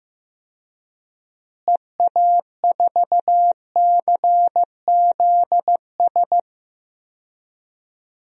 Morse code, heard as a keyed tone, about 15 wpm, 700 Hz. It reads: EA4CZS
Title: DXCC-100 Call Signs - 6 Characters - Repeated 3x - 15wpm